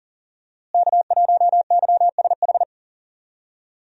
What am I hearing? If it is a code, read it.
Morse code, heard as a keyed tone, about 40 wpm, 700 Hz. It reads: K1YSH
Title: USA Call Signs - 5 Characters - Repeated 2x - 40wpm